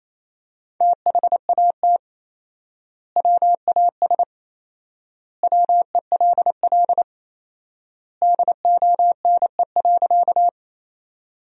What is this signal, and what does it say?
Morse code, heard as a keyed tone, about 28 wpm, 700 Hz. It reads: THAT WAS WELL DONE.
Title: 4x Word-Spacing - Sentences from Top 500 Words - 28wpm